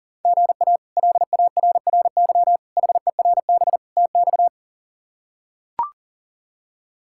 Morse code, heard as a keyed tone, about 40 wpm, 690 Hz. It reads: GA LARRY HERB TX
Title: K1USN SST QSO Practice - 40wpm